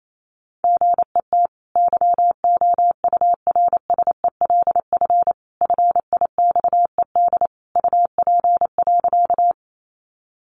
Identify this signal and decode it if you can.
Morse code, heard as a keyed tone, about 28 wpm, 700 Hz. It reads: GET YOURSELF FIXED UP.
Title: Sentences from Top 1100 Words 28wpm